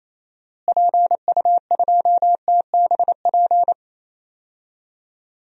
Morse code, heard as a keyed tone, about 28 wpm, 700 Hz. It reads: PU2TBP
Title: DXCC-100 Call Signs - 6 Characters - Repeated 2x - 28wpm